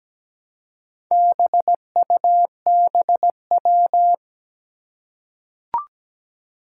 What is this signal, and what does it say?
Morse code, heard as a keyed tone, about 17 wpm, 700 Hz. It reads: BUBW
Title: Four Challenging Letter-Number Combinations 17wpm